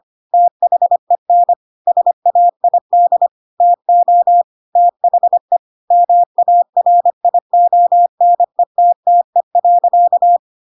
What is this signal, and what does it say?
Morse code, heard as a keyed tone, about 25 wpm, 700 Hz. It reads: THEN SAID TO THE MARIONETTE.
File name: Pinocchio / Pinocchio ch8 25 25 ik1hge